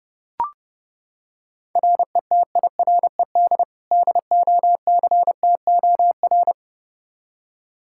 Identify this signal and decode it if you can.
Morse code, heard as a keyed tone, about 30 wpm, 700 Hz. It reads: RETIRED DOCTOR
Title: K1USN SST QSO Practice 30wpm